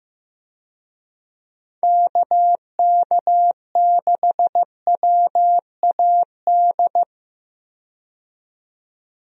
Morse code, heard as a keyed tone, about 15 wpm, 700 Hz. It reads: KK6WAD
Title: USA Call Signs - 6 Characters - 15wpm